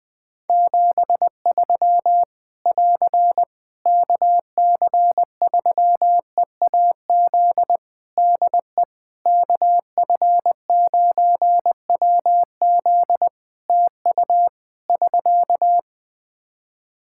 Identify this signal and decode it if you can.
Morse code, heard as a keyed tone, about 20 wpm, 700 Hz. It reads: 73 <AR> KC3EAZ DE KF9WZ TU <SK>
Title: Practice QSO - Exchange 4 - Ending - 20wpm